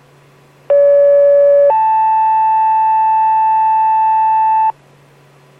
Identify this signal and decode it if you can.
Two-tone page, heard as a one-second tone, then a three-second tone, about 570 then 880 Hz, with noise. It is heard from 0.7 to 4.7 s.